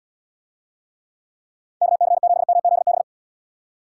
Morse code, heard as a keyed tone, about 75 wpm, 700 Hz. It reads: KC7MZB